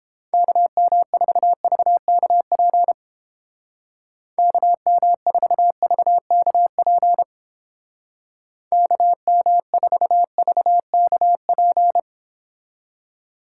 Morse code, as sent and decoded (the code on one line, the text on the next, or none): -.- -- ....- ...- -.- .--. / -.- -- ....- ...- -.- .--. / -.- -- ....- ...- -.- .--.
KM4VKP KM4VKP KM4VKP